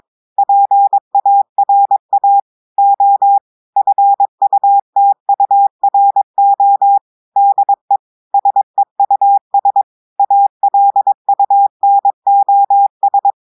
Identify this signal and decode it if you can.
Morse code, heard as a keyed tone, about 22 wpm, 800 Hz. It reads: PARA O FUTURO DE SEUS ALUNOS
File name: audios / portugais / portugais texte en clair 1 vitesse 22 mots minutes